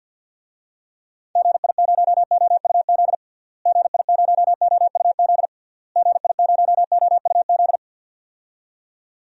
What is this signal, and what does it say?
Morse code, heard as a keyed tone, about 51 wpm, 700 Hz. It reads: GI0OUZ GI0OUZ GI0OUZ
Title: DXCC-100 Call Signs - 6 Characters - Repeated 3x - 50wpm